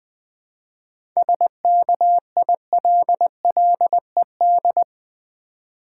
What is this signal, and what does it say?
Morse code, heard as a keyed tone, about 20 wpm, 700 Hz. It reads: SKILLED
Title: 7 Letter Words - First 3 Letters Given - 20wpm